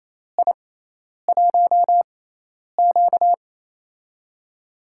Morse code, heard as a keyed tone, about 28 wpm, spaced as for 10 wpm, 700 Hz. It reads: I1Q
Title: USA Call Signs - 4 Characters - 6x Character Spacing - 28wpm